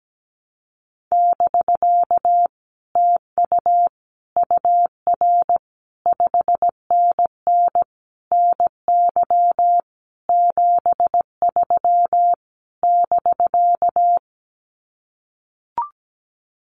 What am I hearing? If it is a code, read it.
Morse code, heard as a keyed tone, about 17 wpm, 700 Hz. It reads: <BK> TU UR 5NN NY 73 <BK>
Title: Common QSO Phrases Encore 17wpm